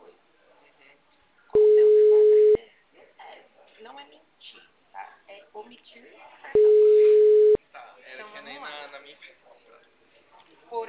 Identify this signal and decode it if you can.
Ringback tone (Europe, 425 Hz), with speech.